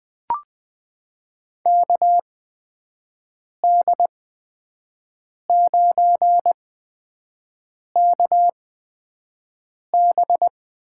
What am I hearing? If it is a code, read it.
Morse code, heard as a keyed tone, about 20 wpm, spaced as for 5 wpm, 700 Hz. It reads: KD9KB
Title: USA Call Signs - 6 Characters - 8x Character Spacing - 20wpm